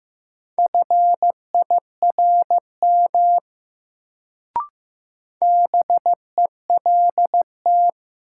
Morse code, heard as a keyed tone, about 15 wpm, 700 Hz. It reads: FIRM BELT
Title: Sentences from Top 1400 Words - New Word Drill - 15wpm